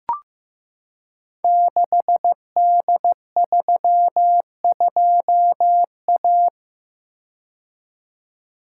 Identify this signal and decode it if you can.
Morse code, heard as a keyed tone, about 15 wpm, 700 Hz. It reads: 6D32A
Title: Code Groups - 5 Letter:Numbers - 15wpm